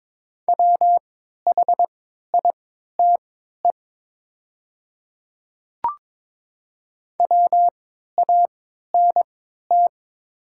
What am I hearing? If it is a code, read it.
Morse code, heard as a keyed tone, about 22 wpm, spaced as for 12 wpm, 700 Hz. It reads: WHITE WANT
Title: 3x Character Spacing - Top 300 Words - 22wpm